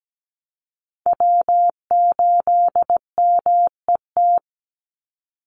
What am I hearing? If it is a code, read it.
Morse code, heard as a keyed tone, about 17 wpm, 700 Hz. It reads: W8MET
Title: Call Signs Repeated 2x - 17wpm